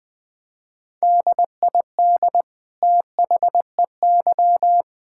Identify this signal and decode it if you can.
Morse code, heard as a keyed tone, about 20 wpm, 700 Hz. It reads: DID THEY
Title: Sentences from Top 1000 Words - Review Part 5 - 20wpm